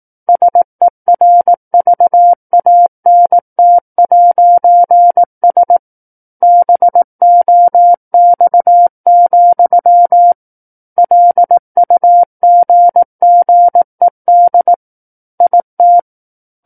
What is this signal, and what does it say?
Morse code, heard as a keyed tone, about 18 wpm, 700 Hz. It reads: SERVANT'S BOX, LUGGED IT
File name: 18 WPM